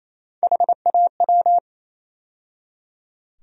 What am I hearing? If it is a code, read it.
Morse code, heard as a keyed tone, about 28 wpm, 700 Hz. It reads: HAW
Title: USA Call Sign Postfixes - 3 Letters 28wpm